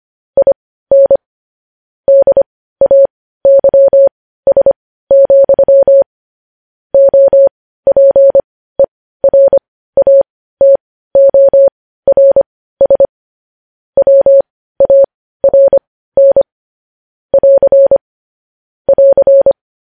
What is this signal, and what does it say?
Morse code, heard as a keyed tone, about 25 wpm, spaced as for 15 wpm, 550 Hz. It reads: IN DAYS, OPERATORS WARN <AR> <AR>
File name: podcast / News headlines-1x-25x15